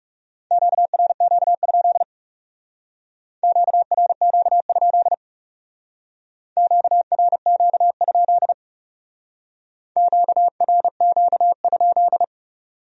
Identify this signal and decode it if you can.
Morse code, heard as a keyed tone, about 38 wpm, 700 Hz. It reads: QRQ? QRQ? QRQ? QRQ?